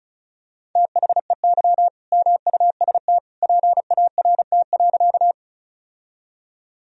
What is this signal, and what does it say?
Morse code, heard as a keyed tone, about 36 wpm, 700 Hz. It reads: THEY MUST PART.
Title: Sentences from Top 1000 Words - Review Part 1 - 35wpm